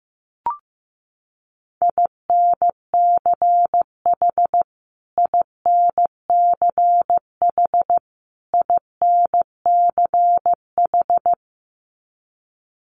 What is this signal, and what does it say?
Morse code, heard as a keyed tone, about 15 wpm, 700 Hz. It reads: INCH INCH INCH
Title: Top 500 Words Repeated 15wpm